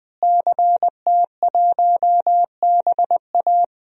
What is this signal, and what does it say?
Morse code, heard as a keyed tone, about 20 wpm, 700 Hz. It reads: CT1BA
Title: DXCC-100 Call Signs - 5 Characters - Repeated 3x - 20wpm